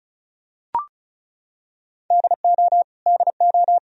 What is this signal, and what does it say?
Morse code, heard as a keyed tone, about 36 wpm, 700 Hz. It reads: DO DO
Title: Top 100 Words Repeated 35wpm